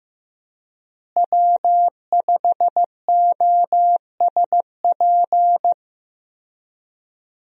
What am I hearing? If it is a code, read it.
Morse code, heard as a keyed tone, about 15 wpm, 700 Hz. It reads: W5OSP
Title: USA Call Signs - 5 Characters - Repeated 3x - 15wpm